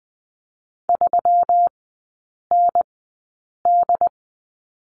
Morse code, heard as a keyed tone, about 20 wpm, spaced as for 8 wpm, 700 Hz. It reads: 3ND
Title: 166 Lesson 27e - Number 7 - Three Characters 20wpm